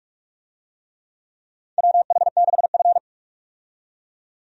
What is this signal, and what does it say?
Morse code, heard as a keyed tone, about 42 wpm, 700 Hz. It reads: WH6F
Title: USA Call Signs - 4 Characters - 45wpm